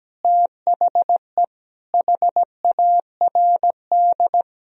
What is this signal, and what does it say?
Morse code, heard as a keyed tone, about 17 wpm, 700 Hz. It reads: THE HARD
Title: Sentences from Top 1000 Words - Review Part 2 - 17wpm